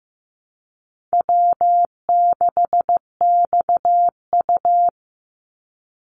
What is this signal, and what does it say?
Morse code, heard as a keyed tone, about 15 wpm, 700 Hz. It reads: W6XU
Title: Easy Call Signs Repeated 3x - 15wpm